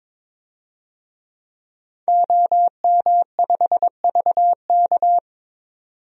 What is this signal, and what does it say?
Morse code, heard as a keyed tone, about 22 wpm, 700 Hz. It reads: OM5VK
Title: DXCC-100 Call Signs - 5 Characters - 22wpm